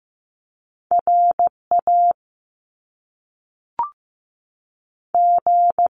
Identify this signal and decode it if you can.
Morse code, heard as a keyed tone, about 15 wpm, spaced as for 5 wpm, 700 Hz. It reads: RAG
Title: Top Two Letter Combinations in Words 15wpm